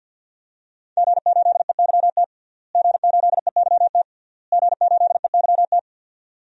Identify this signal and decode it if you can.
Morse code, heard as a keyed tone, about 52 wpm, 700 Hz. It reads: G8EYT G8EYT G8EYT
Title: DXCC-100 Call Signs - 5 Characters - Repeated 3x - 50wpm